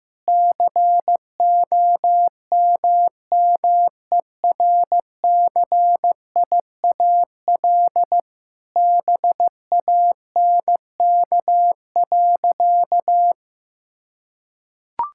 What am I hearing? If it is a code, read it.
Morse code, heard as a keyed tone, about 15 wpm, 700 Hz. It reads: COMMERCIAL BANK.
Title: Sentences from Top 1000 Words - Review Part 6 - 15wpm